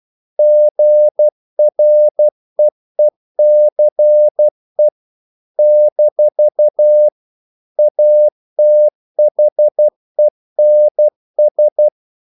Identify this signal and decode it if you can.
Morse code, heard as a keyed tone, about 12 wpm, 600 Hz. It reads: GREECE - ATHENS
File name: CW / Practice / NationalCapitals12